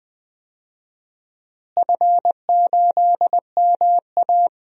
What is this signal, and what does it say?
Morse code, heard as a keyed tone, about 20 wpm, 700 Hz. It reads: F8MA
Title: DXCC-100 Call Signs - 4 Characters - Repeated 2x - 20wpm